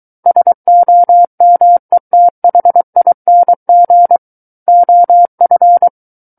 Morse code, heard as a keyed tone, about 23 wpm, 700 Hz. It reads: SOMETHING OF
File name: Book One / 14-Chapter 14 - 23 WPM